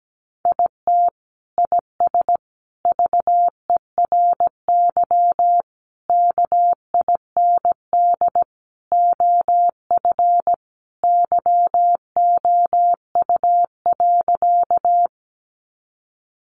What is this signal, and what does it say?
Morse code, heard as a keyed tone, about 17 wpm, 700 Hz. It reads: IT IS VERY KIND OF YOU.
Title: Sentences from Top 200 Words 17wpm